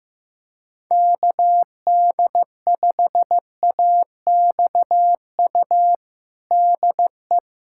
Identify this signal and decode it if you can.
Morse code, heard as a keyed tone, about 15 wpm, 700 Hz. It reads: KD5AXU DE
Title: USA Callsign DE Callsign - 6 Characters - 15wpm